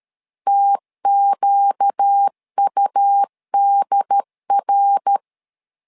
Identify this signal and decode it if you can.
Morse code, heard as a keyed tone, about 13 wpm, 800 Hz. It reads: TQUDR
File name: LEZIONI / 4 Q1S3 12wpm 2023